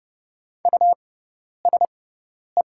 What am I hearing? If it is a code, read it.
Morse code, heard as a keyed tone, about 31 wpm, spaced as for 10 wpm, 700 Hz. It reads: USE
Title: 6x Character Spacing - Top 200 Words - 30wpm